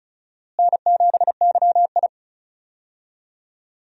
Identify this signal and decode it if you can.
Morse code, heard as a keyed tone, about 34 wpm, 700 Hz. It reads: N7YI
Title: USA Call Signs - 4 Characters - Repeated 2x - 35wpm